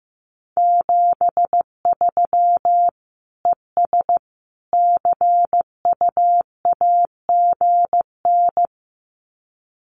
Morse code, heard as a keyed tone, about 15 wpm, 700 Hz. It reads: 73 ES CUAGN
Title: Common QSO Phrases 15wpm